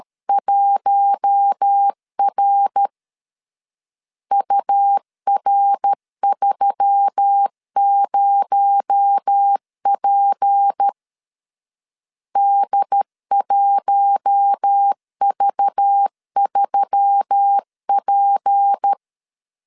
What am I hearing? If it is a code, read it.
Morse code, heard as a keyed tone, about 13 wpm, 800 Hz. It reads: 1R UR30P D1V3P
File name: LEZIONI / 6 PKX0 / 6 PKX0 12wpm 2231